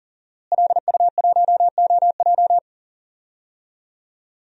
Morse code, heard as a keyed tone, about 37 wpm, 690 Hz. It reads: LU1OJ
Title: DXCC-100 Call Signs - 5 Characters - Repeated 2x - 40wpm